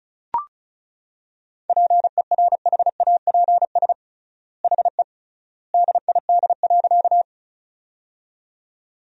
Morse code, heard as a keyed tone, about 34 wpm, 700 Hz. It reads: PERHAPS HE DID.
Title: 3x Word-Spacing - Sentences from Top 500 Words - 35wpm